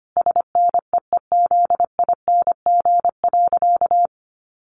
Morse code, heard as a keyed tone, about 25 wpm, 700 Hz. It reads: SNEEZING.
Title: Easy Phrases 25wpm